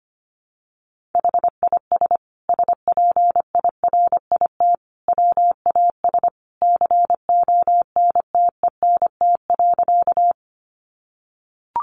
Morse code, heard as a keyed tone, about 25 wpm, 700 Hz. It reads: HIS SPIRIT WAS CONTENT.
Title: Sentences from Top 1400 Words 25wpm